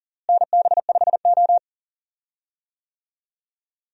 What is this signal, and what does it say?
Morse code, heard as a keyed tone, about 40 wpm, 700 Hz. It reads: NB5O
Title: USA Call Signs - 4 Characters - Repeated 3x - 40wpm